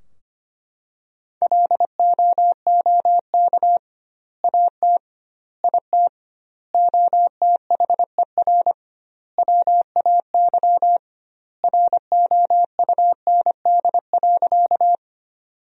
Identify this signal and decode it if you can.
Morse code, heard as a keyed tone, about 25 wpm, 700 Hz. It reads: LOOK AT IT OTHER WAY ROUND.
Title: 2x Word-Spacing - Sentences from Top 200 Words - 25wpm